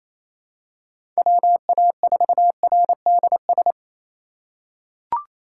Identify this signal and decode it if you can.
Morse code, heard as a keyed tone, about 28 wpm, 700 Hz. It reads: WA4RDS
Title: USA Call Signs - 6 Characters - 2x Character Spacing - 28wpm